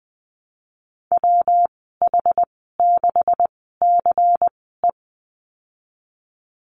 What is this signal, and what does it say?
Morse code, heard as a keyed tone, about 20 wpm, spaced as for 15 wpm, 700 Hz. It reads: WH6CE
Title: 173 Lesson 28g - Number 9 - Call Signs 20wpm